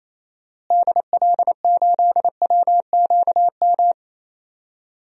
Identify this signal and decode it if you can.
Morse code, heard as a keyed tone, about 28 wpm, 700 Hz. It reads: DL8WQM
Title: DXCC-100 Call Signs - 6 Characters - Repeated 2x - 28wpm